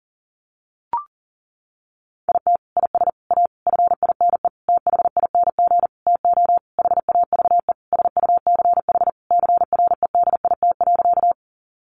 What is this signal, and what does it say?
Morse code, heard as a keyed tone, about 40 wpm, 700 Hz. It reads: IT IS A FINE THING TO HAVE SUCH CREDIT.
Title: Sentences from Top 1000 Words 40wpm